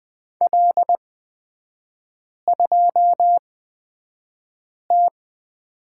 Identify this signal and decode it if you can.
Morse code, heard as a keyed tone, about 20 wpm, spaced as for 5 wpm, 700 Hz. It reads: L2T
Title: 224 Lesson 36e - BT - Three Characters 20wpm